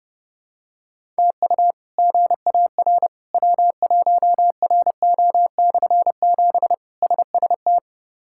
Morse code, heard as a keyed tone, about 30 wpm, 700 Hz. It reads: TU GAR W1RO/7 SST
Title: K1USN SST QSO Practice - 30wpm